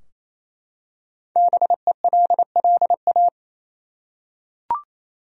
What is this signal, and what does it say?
Morse code, heard as a keyed tone, about 28 wpm, 700 Hz. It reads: BELLA
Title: USA Common Names - Max 5 Letters - 28wpm